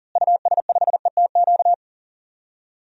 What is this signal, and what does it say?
Morse code, heard as a keyed tone, about 40 wpm, 700 Hz. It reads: US5ETQ
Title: DXCC-100 Call Signs - 6 Characters - Repeated 3x - 40wpm